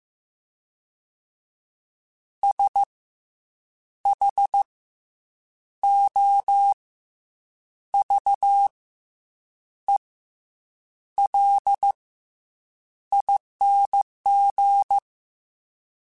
Morse code, heard as a keyed tone, about 15 wpm, 800 Hz. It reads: S H O V E L ING
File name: Special INGs / Special INGs-mp3s / SlowToFaster ing @13